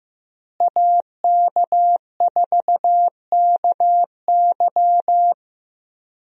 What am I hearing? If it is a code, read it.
Morse code, heard as a keyed tone, about 16 wpm, 700 Hz. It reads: AK4KY
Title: USA Call Signs - 5 Characters - Repeated 3x - 15wpm